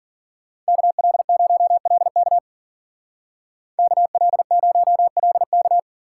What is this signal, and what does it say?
Morse code, heard as a keyed tone, about 52 wpm, 700 Hz. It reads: KL0LK KL0LK